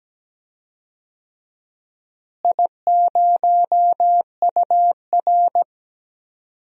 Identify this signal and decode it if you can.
Morse code, heard as a keyed tone, about 17 wpm, 700 Hz. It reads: I0UR
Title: DXCC-100 Call Signs - 4 Characters - Repeated 2x - 17wpm